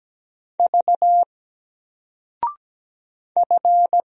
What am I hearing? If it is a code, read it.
Morse code, heard as a keyed tone, about 17 wpm, 700 Hz. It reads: V F
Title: Single Letter-Number 17wpm